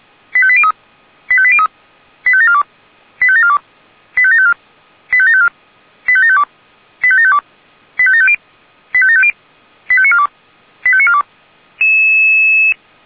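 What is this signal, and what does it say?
Five-tone selective call (ZVEI), heard as quick runs of five tones, tones of about 70 ms. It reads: 85803 85803 85742 85742 85754 85754 85752 85752 85750 85750 85942 85942
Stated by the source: ZVEI five-tone selective call received on SDR.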